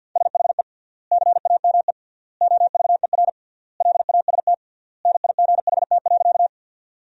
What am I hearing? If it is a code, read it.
Morse code, heard as a keyed tone, about 50 wpm, 700 Hz. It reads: SHE CAME OVER LAST NIGHT.
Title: 3x Word-Spacing - Sentences from Top 300 Words - 50wpm